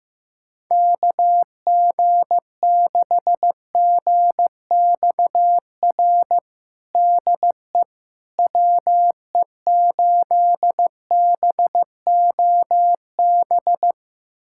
Morse code, heard as a keyed tone, about 15 wpm, 700 Hz. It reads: KG6GXR DE WE8BOB
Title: USA Callsign DE Callsign 15wpm